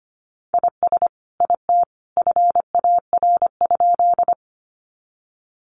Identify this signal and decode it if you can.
Morse code, heard as a keyed tone, about 25 wpm, 700 Hz. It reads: IS IT FAR?
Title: Sentences from Top 300 Words 25wpm